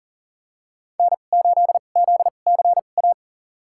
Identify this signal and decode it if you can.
Morse code, heard as a keyed tone, about 39 wpm, spaced as for 28 wpm, 700 Hz. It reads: N8ZCA
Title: USA Call Signs - 5 Characters - 2x Character Spacing - 40wpm